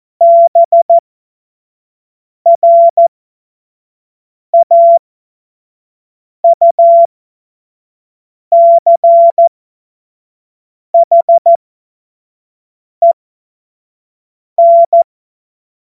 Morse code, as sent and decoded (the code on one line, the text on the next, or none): -... .-. .- ..- -.-. .... -.
BRAUCHEN